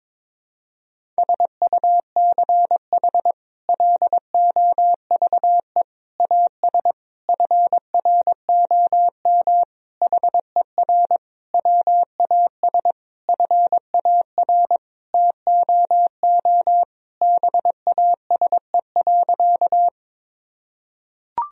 Morse code, heard as a keyed tone, about 21 wpm, 700 Hz. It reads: SUCH LOVE AS FROM HER WAS FAR TOO BASE.
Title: Sentences from Top 1000 Words - Review Part 1 - 22wpm